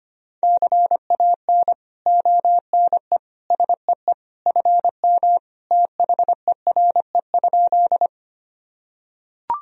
Morse code, heard as a keyed tone, about 25 wpm, 700 Hz. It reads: CAN ONE SEE FM THERE?
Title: Sentences from Top 100 Words and Abbreviations 25wpm